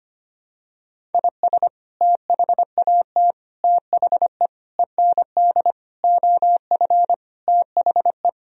Morse code, heard as a keyed tone, about 25 wpm, 700 Hz. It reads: IS THAT THE END OF THE
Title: Sentences from Top 1000 Words - Review Part 1 - 25wpm